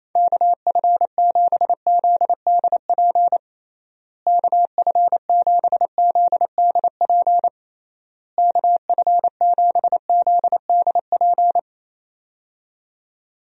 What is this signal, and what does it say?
Morse code, heard as a keyed tone, about 28 wpm, 700 Hz. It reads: KF7ZDP KF7ZDP KF7ZDP
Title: USA Call Signs - 6 Characters - Repeated 3x - 28wpm